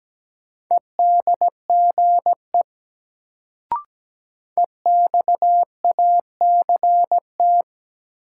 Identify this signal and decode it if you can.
Morse code, heard as a keyed tone, about 17 wpm, 700 Hz. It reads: EDGE EXACT
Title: Sentences from Top 1300 Words - New Word Drill - 17wpm